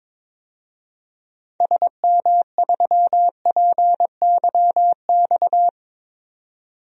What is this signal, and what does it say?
Morse code, heard as a keyed tone, about 21 wpm, 700 Hz. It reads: SM3PYX
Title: DXCC-100 Call Signs - 6 Characters - 22wpm